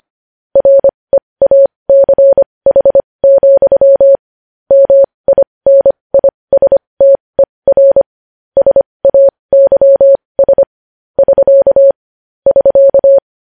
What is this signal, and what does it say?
Morse code, heard as a keyed tone, about 25 wpm, spaced as for 20 wpm, 550 Hz. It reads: REACH, MINISTER SAYS <SK> <SK>
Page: news headlines in morse code at 20 wpm